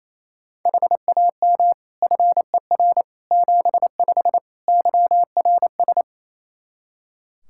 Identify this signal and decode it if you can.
Morse code, heard as a keyed tone, about 28 wpm, 700 Hz. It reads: HAM FER 75 YRS
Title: K1USN SST QSO Practice 28wpm